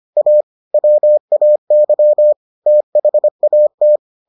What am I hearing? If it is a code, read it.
Morse code, heard as a keyed tone, about 25 wpm, 600 Hz. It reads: A WAY THAT